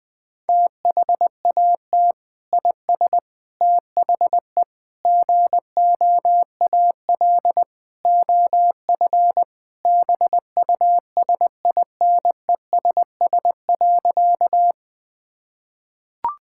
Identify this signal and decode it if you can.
Morse code, heard as a keyed tone, about 20 wpm, 700 Hz. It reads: THAT IS THE GOAL OF BUSINESS.
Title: Sentences from Top 1000 Words - Review Part 5 - 20wpm